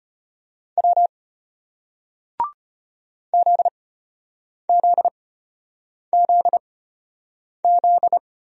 Morse code, heard as a keyed tone, about 35 wpm, spaced as for 8 wpm, 700 Hz. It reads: W ZZZZ